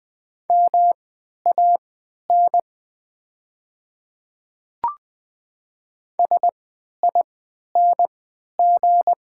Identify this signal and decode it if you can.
Morse code, heard as a keyed tone, about 20 wpm, spaced as for 11 wpm, 700 Hz. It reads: MAN SING
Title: 3x Character Spacing - Top 500 Words - 20wpm